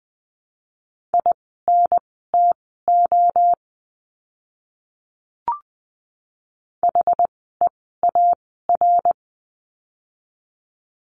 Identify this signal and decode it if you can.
Morse code, heard as a keyed tone, about 20 wpm, spaced as for 14 wpm, 700 Hz. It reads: INTO HEAR
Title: 206 Lesson 33f - Letter j - Words 20wpm